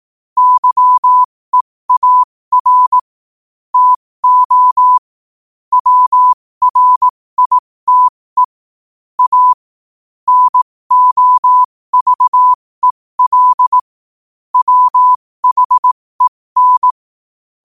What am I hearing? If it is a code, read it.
Morse code, heard as a keyed tone, about 18 wpm, spaced as for 15 wpm, 1000 Hz. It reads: YEAR TO WRITE A NOVEL WHEN
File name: qotdfiles / 15wpm